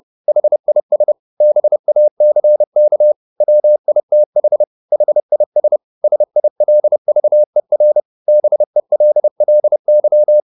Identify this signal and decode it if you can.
Morse code, heard as a keyed tone, about 30 wpm, 600 Hz. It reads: HIS BACK WITH HIS SILVER BELLY